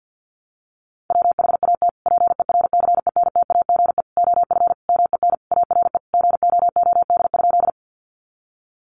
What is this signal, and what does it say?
Morse code, heard as a keyed tone, about 50 wpm, 700 Hz. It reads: WHAT PERCENTAGE OF MEN ARE GOOD?
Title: Sentences from Top 1300 Words 50wpm